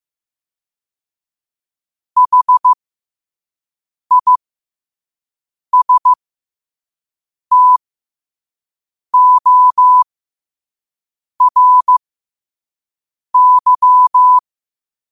Morse code, heard as a keyed tone, about 15 wpm, spaced as for 5 wpm, 1000 Hz. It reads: HISTORY
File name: qotdfiles / 5wpm